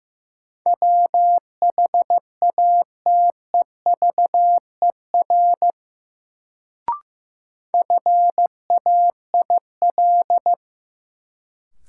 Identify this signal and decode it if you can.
Morse code, heard as a keyed tone, about 15 wpm, 700 Hz. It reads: WHATEVER FAIL
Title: Sentences from Top 1100 Words - New Word Drill Alt - 15wpm